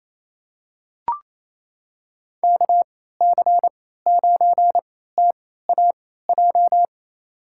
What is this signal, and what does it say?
Morse code, heard as a keyed tone, about 28 wpm, spaced as for 16 wpm, 700 Hz. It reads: KC9TAJ
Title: USA Call Signs - 6 Characters - 3x Character Spacing - 28wpm